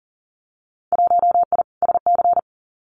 Morse code, heard as a keyed tone, about 40 wpm, 700 Hz. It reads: 1I SC
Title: Winter Field Day Exchanges Encore 40wpm